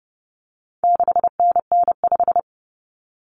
Morse code, heard as a keyed tone, about 30 wpm, 700 Hz. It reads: 6NN5
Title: Common Cut Numbers - 4-Digit 30wpm